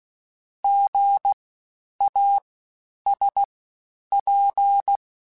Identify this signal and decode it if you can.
Morse code, heard as a keyed tone, about 16 wpm, 800 Hz. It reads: G A S P
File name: Special INGs-mp3s / Special INGs-mp3s / SlowToFaster ing @15